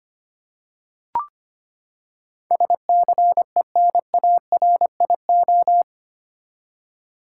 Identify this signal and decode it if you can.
Morse code, heard as a keyed tone, about 25 wpm, 700 Hz. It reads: SCENARIO
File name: Eight Letter Words - 25wpm